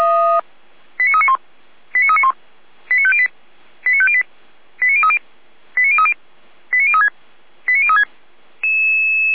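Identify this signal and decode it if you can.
Five-tone selective call (ZVEI), heard as quick runs of five tones, tones of about 70 ms. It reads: 89391 89391 89508 89508 89030 89030 89036 89036
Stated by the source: Five-tone ZVEI selective call received on SDR.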